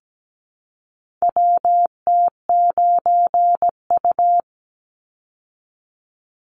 Morse code, heard as a keyed tone, about 17 wpm, 700 Hz. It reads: WT9U
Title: Call Signs Repeated 3x - 17wpm